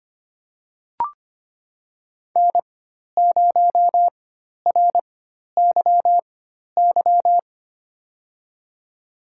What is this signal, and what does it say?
Morse code, heard as a keyed tone, about 25 wpm, spaced as for 12 wpm, 700 Hz. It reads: N0RYY
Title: USA Call Signs - 4x Character Spacing - 25wpm